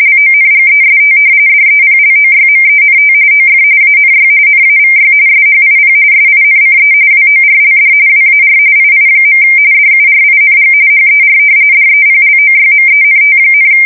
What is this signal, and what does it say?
RTTY, about 45.45 baud. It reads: LONGEST DEPLOYMENT OF ANY NAVY SHIP SINCE WORLD WAR II. DURING HER DEPLOYMENT,
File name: fsk170 / battleshipnj